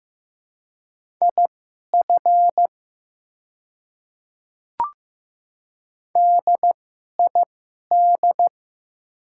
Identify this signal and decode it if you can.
Morse code, heard as a keyed tone, about 15 wpm, spaced as for 11 wpm, 700 Hz. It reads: IF DID
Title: 2x Character Spacing - Top 100 Words - 15wpm